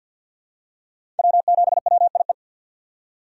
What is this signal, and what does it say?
Morse code, heard as a keyed tone, about 50 wpm, 700 Hz. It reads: W7WIE
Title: USA Call Signs - 2x Character Spacing - 50wpm